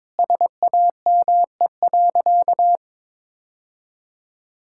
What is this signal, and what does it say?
Morse code, heard as a keyed tone, about 22 wpm, 700 Hz. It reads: SAME.
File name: practice / 22wpm / Sentences with 2-Word Phrases - 22wpm